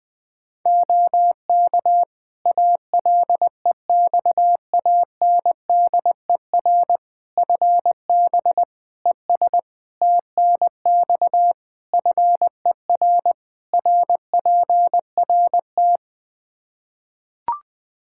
Morse code, heard as a keyed tone, about 20 wpm, 700 Hz. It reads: OK ALEXANDER FB ES TNX FER RPRT
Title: Practice QSO - Exchange 2 - Info - 20wpm